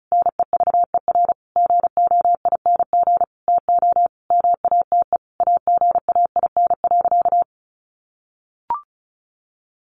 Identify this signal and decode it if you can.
Morse code, heard as a keyed tone, about 34 wpm, 700 Hz. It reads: NEVER GOING TO MATE AGAIN.
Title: A Sound Of Thunder 35wpm